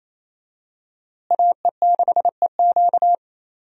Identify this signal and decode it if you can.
Morse code, heard as a keyed tone, about 27 wpm, 700 Hz. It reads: AE6EQ
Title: USA Call Signs - 5 Characters - Repeated 3x - 28wpm